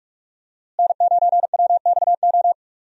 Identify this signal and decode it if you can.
Morse code, heard as a keyed tone, about 46 wpm, 700 Hz. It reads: N9WXO
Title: USA Call Signs - 5 Characters - Repeated 3x - 45wpm